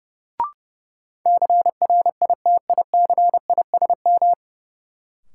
Morse code, heard as a keyed tone, about 30 wpm, 700 Hz. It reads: CRITICISM
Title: Sentences from Top 1300 Words - New Word Drill - 30wpm